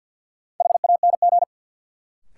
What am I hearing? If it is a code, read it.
Morse code, heard as a keyed tone, about 47 wpm, 700 Hz. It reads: HANG
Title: Sentences from Top 1600 Words - New Word Drill - 50wpm